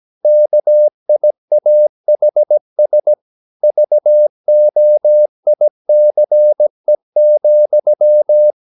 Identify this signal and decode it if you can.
Morse code, heard as a keyed tone, about 17 wpm, 600 Hz. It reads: KIAHS VOICE,
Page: Romanen Never af Ken Follet på engelsk.